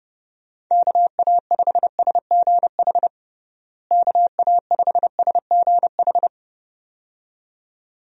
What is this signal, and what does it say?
Morse code, heard as a keyed tone, about 31 wpm, 700 Hz. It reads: KA5SGH KA5SGH
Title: USA Call Signs - 6 Characters - Repeated 2x - 30wpm